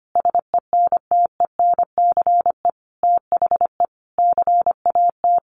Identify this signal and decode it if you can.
Morse code, heard as a keyed tone, about 25 wpm, 700 Hz. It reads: SENTENCE THE CAT
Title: Additional Easy Phrases 25wpm